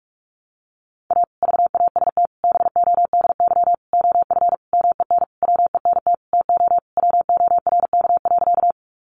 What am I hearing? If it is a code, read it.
Morse code, heard as a keyed tone, about 44 wpm, 700 Hz. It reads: A VAST BODY OF MEN WENT TO WORK.
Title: Sentences from Top 1300 Words 45wpm